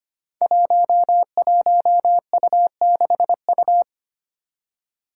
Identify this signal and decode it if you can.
Morse code, heard as a keyed tone, about 25 wpm, 700 Hz. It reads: 11U6U
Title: Five Challenging Letter-Number Combinations 25wpm